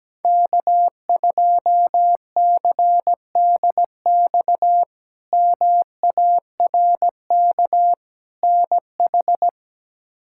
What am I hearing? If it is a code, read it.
Morse code, heard as a keyed tone, about 17 wpm, 700 Hz. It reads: K2CDX MARK NH
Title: K1USN SST QSO Practice - 17wpm